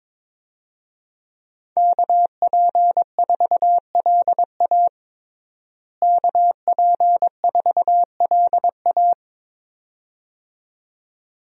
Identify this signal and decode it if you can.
Morse code, heard as a keyed tone, about 22 wpm, 700 Hz. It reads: KP4LA KP4LA
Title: USA Call Signs - 5 Characters - Repeated 2x - 22wpm